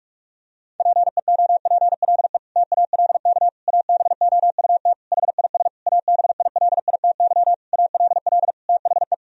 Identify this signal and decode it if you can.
Morse code, heard as a keyed tone, about 46 wpm, 700 Hz. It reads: PEOPLE TALK ABOUT HIS ABILITY ALL THE
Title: Sentences from Top 1000 Words - Review Part 3 - 45wpm